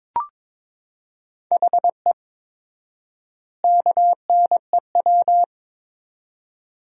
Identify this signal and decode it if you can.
Morse code, heard as a keyed tone, about 22 wpm, 700 Hz. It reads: HE KNEW
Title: 4x Word-Spacing - Sentences from Top 500 Words - 22wpm